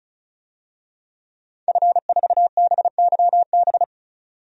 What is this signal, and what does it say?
Morse code, heard as a keyed tone, about 34 wpm, 700 Hz. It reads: F4BYB
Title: DXCC-100 Call Signs - 5 Characters - 35wpm